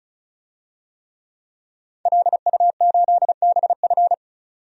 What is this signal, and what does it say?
Morse code, heard as a keyed tone, about 34 wpm, 700 Hz. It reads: LU8BF
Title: DXCC-100 Call Signs - 5 Characters - 35wpm